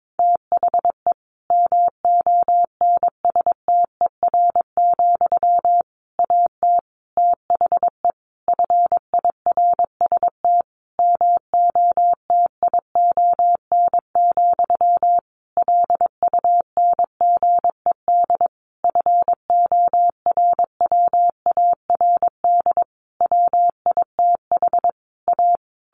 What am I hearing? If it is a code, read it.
Morse code, heard as a keyed tone, about 22 wpm, 700 Hz. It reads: THE MONSTER, AT THE FIRST MOTION, LUNGED FORWARD WITH A
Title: A Sound Of Thunder 22wpm